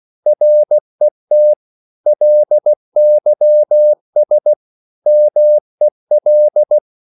Morse code, heard as a keyed tone, about 16 wpm, 600 Hz. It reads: RET LYS MEL
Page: Korte danske ord 16wpm